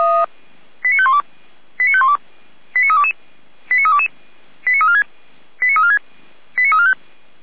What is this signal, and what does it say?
Five-tone selective call (ZVEI), heard as quick runs of five tones, tones of about 70 ms. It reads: 89512 89512 89322 89322 89346 89346 89345
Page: SDR/ZVEI